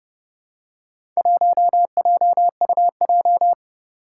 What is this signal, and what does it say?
Morse code, heard as a keyed tone, about 29 wpm, 700 Hz. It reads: 1JUJ
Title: Four Challenging Letter-Number Combinations 30wpm